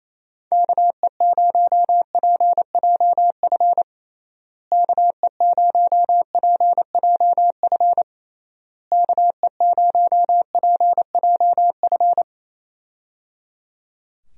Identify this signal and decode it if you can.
Morse code, heard as a keyed tone, about 27 wpm, 700 Hz. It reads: KE0PJF KE0PJF KE0PJF
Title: USA Call Signs - 6 Characters - Repeated 3x - 28wpm